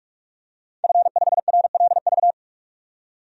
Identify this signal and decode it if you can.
Morse code, heard as a keyed tone, about 41 wpm, 700 Hz. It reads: F5RLV
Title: DXCC-100 Call Signs - 5 Characters - 45wpm